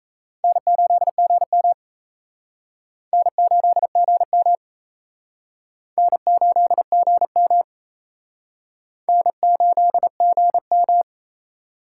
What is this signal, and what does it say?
Morse code, heard as a keyed tone, about 36 wpm, 700 Hz. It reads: N8GM N8GM N8GM N8GM